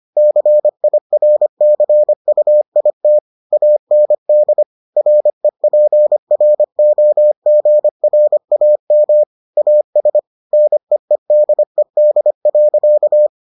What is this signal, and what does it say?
Morse code, heard as a keyed tone, about 25 wpm, 600 Hz. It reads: CIRCUIT AND REPROGRAM AS NEEDED.